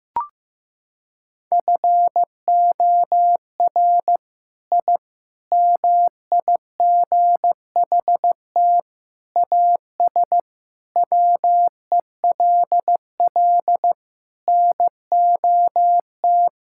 Morse code, heard as a keyed tone, about 15 wpm, 700 Hz. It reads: FOR I MIGHT AS WELL NOT
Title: Sentences from Top 1000 Words - Review Part 5 - 15wpm